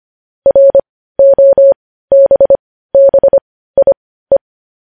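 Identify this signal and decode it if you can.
Morse code, heard as a keyed tone, about 25 wpm, spaced as for 15 wpm, 550 Hz. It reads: ROBBIE
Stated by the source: Morse news headlines practice at 15.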